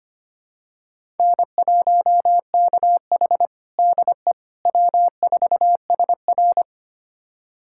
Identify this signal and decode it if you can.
Morse code, heard as a keyed tone, about 25 wpm, 700 Hz. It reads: N1KH DE W4SR
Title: USA Callsign DE Callsign - 4 Characters - 25wpm